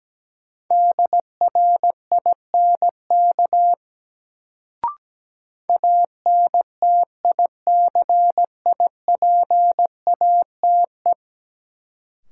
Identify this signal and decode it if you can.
Morse code, heard as a keyed tone, about 17 wpm, 700 Hz. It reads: DRINK ANTICIPATE
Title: Sentences from Top 1500 Words - New Word Drill Alt - 17wpm